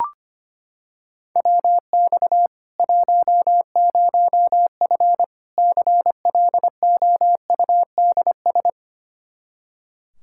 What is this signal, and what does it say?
Morse code, heard as a keyed tone, about 25 wpm, 700 Hz. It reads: WX 10F CLOUDS
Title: Practice QSO - Exchange 2 - Info - 25wpm